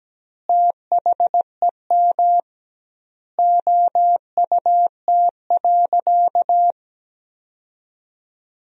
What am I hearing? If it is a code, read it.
Morse code, heard as a keyed tone, about 17 wpm, 700 Hz. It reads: THEM OUT.
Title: 2x Word-Spacing - Sentences from Top 500 Words - 17wpm